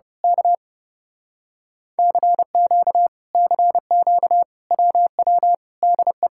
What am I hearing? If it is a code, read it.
Morse code, heard as a keyed tone, about 30 wpm, 700 Hz. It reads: K CQ CQ WW DE
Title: Speed-Racing Common QSO Phrases - 30wpm